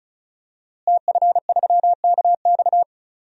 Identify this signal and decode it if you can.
Morse code, heard as a keyed tone, about 34 wpm, 700 Hz. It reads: TF3KX
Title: DXCC-100 Call Signs - 5 Characters - Repeated 2x - 35wpm